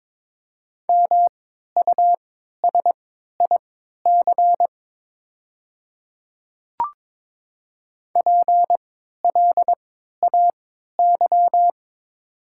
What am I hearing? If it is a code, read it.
Morse code, heard as a keyed tone, about 22 wpm, spaced as for 13 wpm, 700 Hz. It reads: MUSIC PLAY
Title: 3x Character Spacing - Top 300 Words - 22wpm